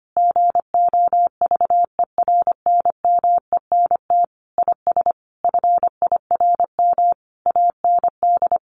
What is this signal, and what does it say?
Morse code, heard as a keyed tone, about 25 wpm, 700 Hz. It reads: GOVERNMENT IS FIRM AND
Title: Sentences from Top 1500 Words 25wpm